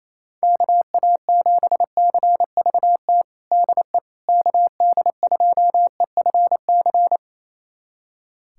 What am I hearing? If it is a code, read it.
Morse code, heard as a keyed tone, about 28 wpm, 700 Hz. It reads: KA7CVT DE KD2EFC
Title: USA Callsign DE Callsign 28wpm